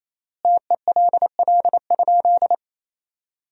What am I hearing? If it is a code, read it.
Morse code, heard as a keyed tone, about 28 wpm, 700 Hz. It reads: TELL?
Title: 3x Word-Spacing - Sentences from Top 200 Words - 28wpm